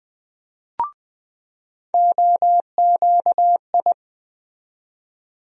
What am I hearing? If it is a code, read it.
Morse code, heard as a keyed tone, about 21 wpm, 700 Hz. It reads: OQI
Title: USA Call Sign Postfixes - 3 Letters 20wpm